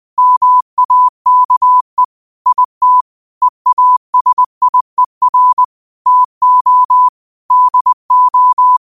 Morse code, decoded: MAKE IT EASIER TO DO

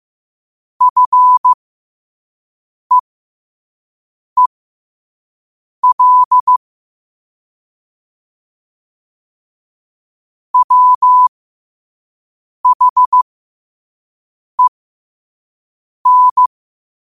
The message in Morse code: FEEL WHEN

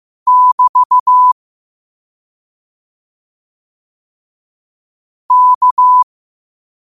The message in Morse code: <BT> K

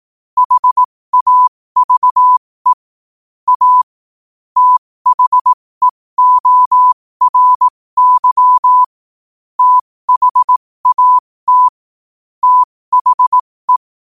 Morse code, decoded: HAVE A THEORY THAT THE

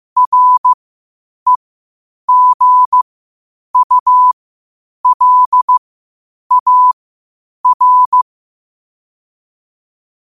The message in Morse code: REGULAR

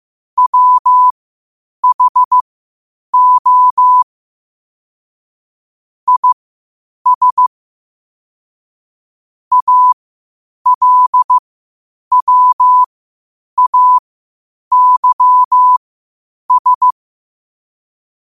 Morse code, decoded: WHO IS ALWAYS

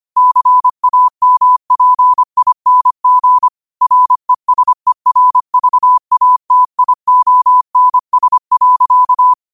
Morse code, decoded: CAMPING RESERVATIONS.

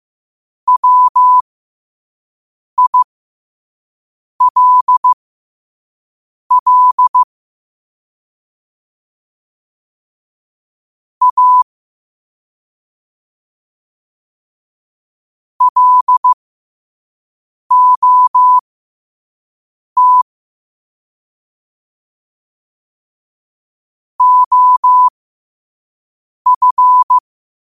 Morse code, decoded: WILL A LOT OF

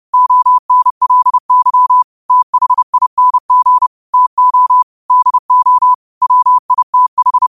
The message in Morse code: ONLY THING TO DO WITH

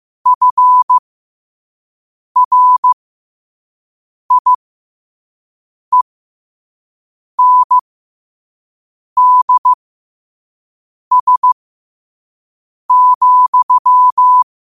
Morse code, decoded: FRIENDS,

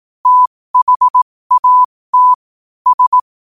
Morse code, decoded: THAT S

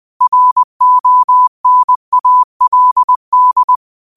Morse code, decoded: RONALD

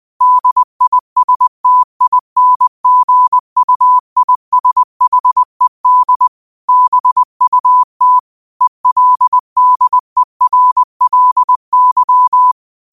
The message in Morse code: DISTINGUISHED BUT ELDERLY

